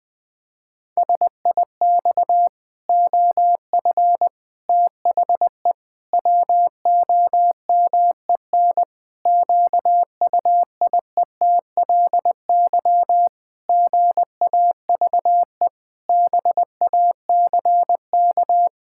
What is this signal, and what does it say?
Morse code, heard as a keyed tone, about 20 wpm, 700 Hz. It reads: SIX OF THE WOMEN QUIETLY GAVE BACK
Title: Pangram Challenge v2 - 20wpm